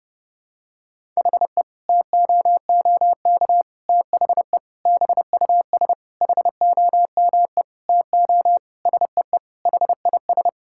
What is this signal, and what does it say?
Morse code, heard as a keyed tone, about 30 wpm, 700 Hz. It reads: HE TOOK THE BUS HOME TO SEE HIS
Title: Sentences from Top 1000 Words - Review Part 6 - 30wpm